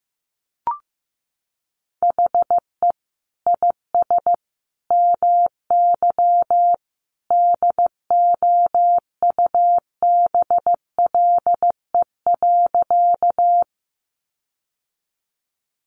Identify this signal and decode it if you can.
Morse code, heard as a keyed tone, about 15 wpm, 700 Hz. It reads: HE IS MY DOUBLE.
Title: Sentences from Top 1400 Words 15wpm